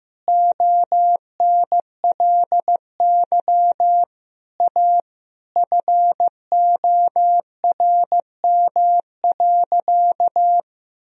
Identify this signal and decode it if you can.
Morse code, heard as a keyed tone, about 15 wpm, 700 Hz. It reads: ONLY A FORM.
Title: Sentences from Top 1000 Words - Review Part 1 - 15wpm